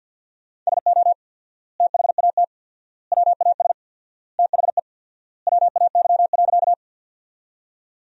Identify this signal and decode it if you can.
Morse code, heard as a keyed tone, about 50 wpm, 700 Hz. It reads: SO THAT WAS THE WAY.
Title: 4x Word-Spacing - Sentences from Top 100 Words - 50wpm